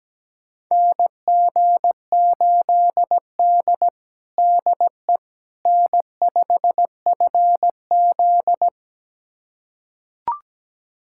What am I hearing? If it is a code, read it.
Morse code, heard as a keyed tone, about 17 wpm, 700 Hz. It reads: NG8D DE N5FZ
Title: USA Callsign DE Callsign - 4 Characters - 17wpm